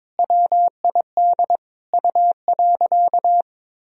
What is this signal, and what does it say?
Morse code, heard as a keyed tone, about 22 wpm, 700 Hz. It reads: WID U.
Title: Sentences from Top 300 Words and Abbreviations 22wpm